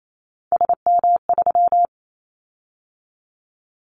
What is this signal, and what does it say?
Morse code, heard as a keyed tone, about 28 wpm, 700 Hz. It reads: SM3
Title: Callsign Prefixes for Top 25 DXCC - 28wpm